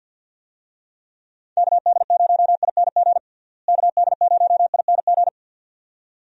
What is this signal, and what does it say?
Morse code, heard as a keyed tone, about 49 wpm, 700 Hz. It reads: KD0ING KD0ING
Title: USA Call Signs - 6 Characters - Repeated 2x - 50wpm